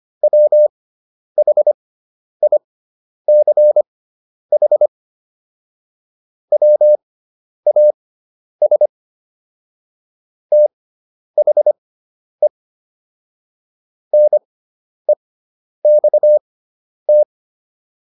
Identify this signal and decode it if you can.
Morse code, heard as a keyed tone, about 25 wpm, spaced as for 10 wpm, 600 Hz. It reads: WHICH WAS THE NEXT